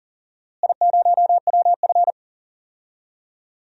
Morse code, heard as a keyed tone, about 40 wpm, 700 Hz. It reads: I0WF
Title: DXCC-100 Call Signs - 4 Characters - Repeated 2x - 40wpm